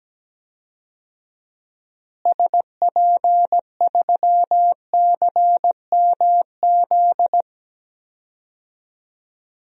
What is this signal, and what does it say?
Morse code, heard as a keyed tone, about 17 wpm, 700 Hz. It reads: SP3CMZ
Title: DXCC-100 Call Signs - 6 Characters - Repeated 2x - 17wpm